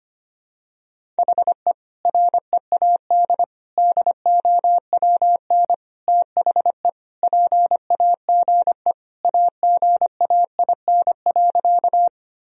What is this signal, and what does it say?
Morse code, heard as a keyed tone, about 25 wpm, 700 Hz. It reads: HE READ DOWN THE PAGE AGAIN.
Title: Sentences from Top 1000 Words - Review Part 1 - 25wpm